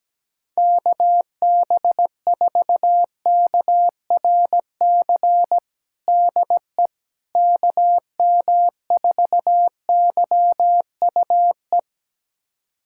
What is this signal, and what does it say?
Morse code, heard as a keyed tone, about 17 wpm, 700 Hz. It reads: KB4KRC DE KM4YUE
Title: USA Callsign DE Callsign - 6 Characters - 17wpm